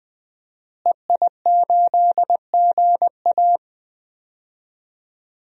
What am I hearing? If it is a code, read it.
Morse code, heard as a keyed tone, about 20 wpm, 700 Hz. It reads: EI8GA
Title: DXCC-100 Call Signs - 5 Characters - Repeated 3x - 20wpm